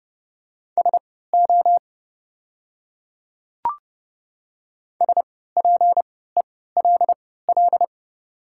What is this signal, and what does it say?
Morse code, heard as a keyed tone, about 30 wpm, spaced as for 17 wpm, 690 Hz. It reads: SO SPELL
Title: 3x Character Spacing - Top 500 Words - 30wpm